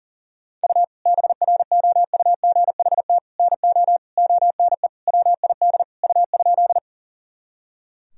Morse code, heard as a keyed tone, about 40 wpm, 700 Hz. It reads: U BROUGHT NO ONE WID U?
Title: Sentences from Top 500 Words and Abbreviations 40wpm